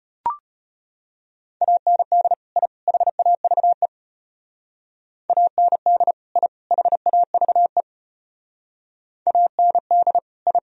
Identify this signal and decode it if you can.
Morse code, heard as a keyed tone, about 30 wpm, 700 Hz. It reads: AND I HAVE AND I HAVE AND I
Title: Speed-Racing - Sets of 3 Words 25wpm